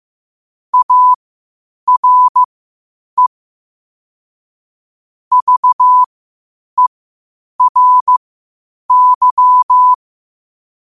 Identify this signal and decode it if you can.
Morse code, heard as a keyed tone, about 15 wpm, spaced as for 8 wpm, 1000 Hz. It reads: ARE VERY
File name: qotdfiles / 8wpm